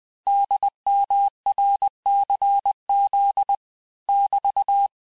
Morse code, heard as a keyed tone, about 20 wpm, 800 Hz. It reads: DMRCZ <BT>
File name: cours-1584-teste-chiffre-vitesse-19mpm